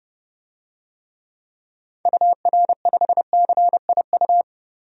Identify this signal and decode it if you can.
Morse code, heard as a keyed tone, about 32 wpm, 700 Hz. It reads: UR5CIU
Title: DXCC-100 Call Signs - 6 Characters - Repeated 3x - 30wpm